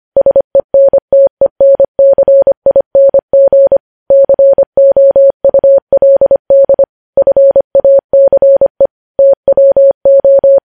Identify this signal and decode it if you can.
Morse code, heard as a keyed tone, about 25 wpm, 550 Hz. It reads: SENTENCING COULD FACE TWO
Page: news headlines in morse code at 25 wpm